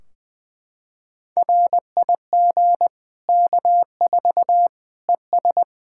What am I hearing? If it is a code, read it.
Morse code, heard as a keyed tone, about 20 wpm, 700 Hz. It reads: RIG K4 ES
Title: Practice QSO - Exchange 2 - Info - 20wpm